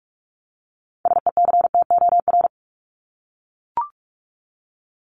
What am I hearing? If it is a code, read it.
Morse code, heard as a keyed tone, about 44 wpm, 700 Hz. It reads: SECTOR
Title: Top 1751-2000 Words 45wpm